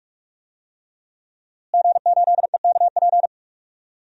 Morse code, heard as a keyed tone, about 43 wpm, 700 Hz. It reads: G8EKP
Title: DXCC-100 Call Signs - 5 Characters - Repeated 3x - 45wpm